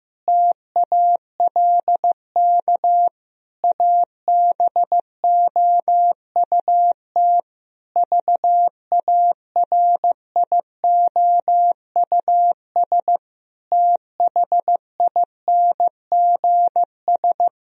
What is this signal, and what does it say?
Morse code, heard as a keyed tone, about 15 wpm, 700 Hz. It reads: TALK ABOUT VARIOUS THINGS
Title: Sentences from Top 1000 Words - Review Part 4 - 15wpm